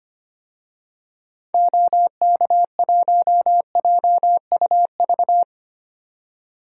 Morse code, heard as a keyed tone, about 25 wpm, 700 Hz. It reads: OK1JUV
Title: DXCC-100 Call Signs - 6 Characters - 25wpm